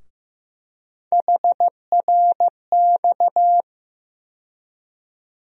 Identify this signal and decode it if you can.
Morse code, heard as a keyed tone, about 15 wpm, 700 Hz. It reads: HRX
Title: USA Call Sign Postfixes - 3 Letters 15wpm